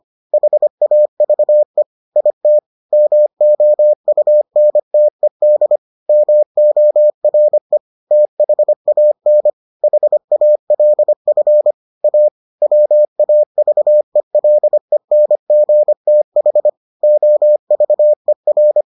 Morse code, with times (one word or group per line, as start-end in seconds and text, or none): 0.3-1.8 HAVE
2.2-2.6 IT
2.9-5.8 MOUNTED
6.1-7.8 MORE
8.1-9.5 THAN
9.8-11.7 HALF
12.0-12.3 A
12.6-16.7 WAVELENGTH
17.0-18.8 OVER